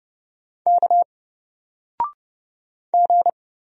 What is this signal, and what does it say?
Morse code, heard as a keyed tone, about 31 wpm, 700 Hz. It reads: K G
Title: Single Letter-Number 30wpm